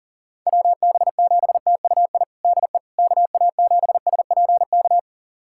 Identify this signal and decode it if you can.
Morse code, heard as a keyed tone, about 40 wpm, 700 Hz. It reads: WB7TUI DE KA7SPK
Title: USA Callsign DE Callsign 40wpm